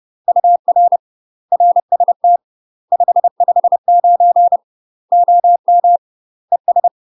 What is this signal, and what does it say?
Morse code, heard as a keyed tone, about 30 wpm, 700 Hz. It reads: UR RST 559 OM ES
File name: UK MASTER QSOs TRAINING / 30WPM / UK MASTER QSOs 0030